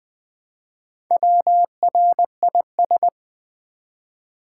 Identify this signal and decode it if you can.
Morse code, heard as a keyed tone, about 20 wpm, 700 Hz. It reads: WRIS
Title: Build A Word - 1 Letter At A Time - 5-Letter Words - 20wpm